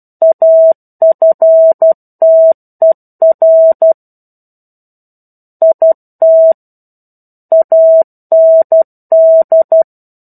Morse code, heard as a keyed tone, about 12 wpm, 650 Hz. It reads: AFTER IT AND
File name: 01-Chapter-01 12 WPM